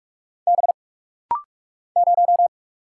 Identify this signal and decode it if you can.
Morse code, heard as a keyed tone, about 45 wpm, 700 Hz. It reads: B 0
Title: Single Letter-Number - Rapid-Fire 45wpm